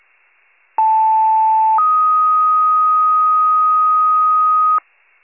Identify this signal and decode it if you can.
Two-tone page, heard as a one-second tone, then a three-second tone, about 880 then 1250 Hz. It is heard from 0.8 to 4.8 s.